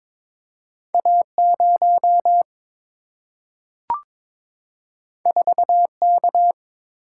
Morse code, heard as a keyed tone, about 22 wpm, 700 Hz. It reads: A0 4K
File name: practice / 22wpm / Two Letter-Number Combinations - 22wpm